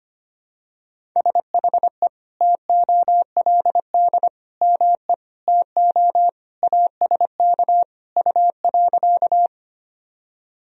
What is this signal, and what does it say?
Morse code, heard as a keyed tone, about 25 wpm, 700 Hz. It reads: SHE TOLD ME TO ASK U.
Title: Sentences from Top 400 Words and Abbreviations 25wpm